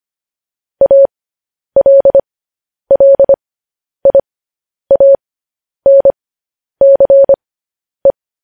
Morse code, heard as a keyed tone, about 25 wpm, spaced as for 10 wpm, 550 Hz. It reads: ALLIANCE
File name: podcast / News headlines-1x-25x10